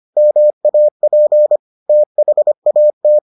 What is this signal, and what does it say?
Morse code, heard as a keyed tone, about 25 wpm, 600 Hz. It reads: MAP THAT